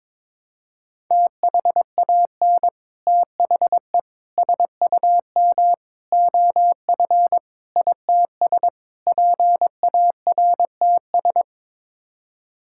Morse code, heard as a keyed tone, about 22 wpm, 700 Hz. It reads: THAN THE SUM OF ITS PARTS
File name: practice / 22wpm / Sets of 6 Words - 22wpm